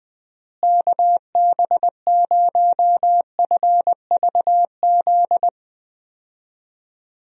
Morse code, heard as a keyed tone, about 20 wpm, 700 Hz. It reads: KB0FVZ
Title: USA Call Signs - 6 Characters - Repeated 3x - 20wpm